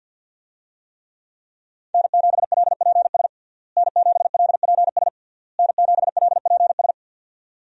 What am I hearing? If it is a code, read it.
Morse code, heard as a keyed tone, about 48 wpm, 690 Hz. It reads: N7LPS N7LPS N7LPS
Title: USA Call Signs - 5 Characters - Repeated 3x - 50wpm